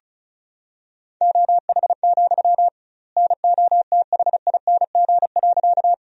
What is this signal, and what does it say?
Morse code, heard as a keyed tone, about 34 wpm, 700 Hz. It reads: OH, NOTHING.
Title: 2x Word-Spacing - Sentences from Top 500 Words - 35wpm